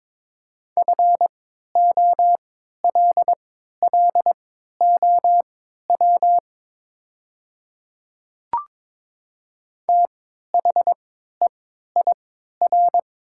Morse code, heard as a keyed tone, about 22 wpm, spaced as for 12 wpm, 700 Hz. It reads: FOLLOW THEIR
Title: 3x Character Spacing - Top 300 Words - 22wpm